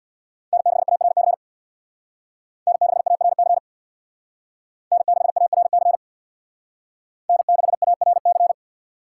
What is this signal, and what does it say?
Morse code, heard as a keyed tone, about 64 wpm, 700 Hz. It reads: N6ARC N6ARC N6ARC N6ARC